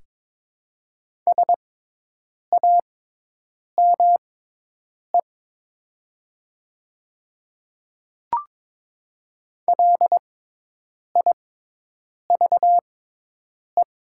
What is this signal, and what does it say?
Morse code, heard as a keyed tone, about 22 wpm, spaced as for 8 wpm, 700 Hz. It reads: SAME LIVE
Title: 6x Character Spacing - Top 500 Words - 22wpm